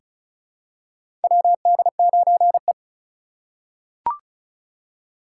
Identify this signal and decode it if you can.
Morse code, heard as a keyed tone, about 34 wpm, 700 Hz. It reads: WD9E
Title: USA Call Signs - 4 Characters - 3x Character Spacing - 35wpm